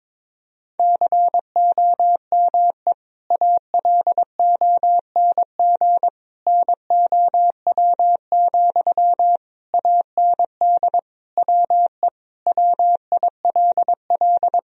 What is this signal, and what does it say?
Morse code, heard as a keyed tone, about 22 wpm, 700 Hz. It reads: COME ALONG NOW, AND WE WILL
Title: Sentences from Top 1000 Words - Review Part 4 - 22wpm